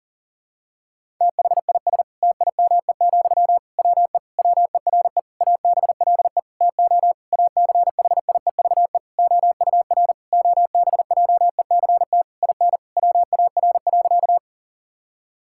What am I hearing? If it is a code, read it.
Morse code, heard as a keyed tone, about 37 wpm, 690 Hz. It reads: THIS TIME, WE WERE ABLE TO ACHIEVE OUR OBJECT IN WAR.
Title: Sentences from Top 1000 Words - Review Part 6 - 40wpm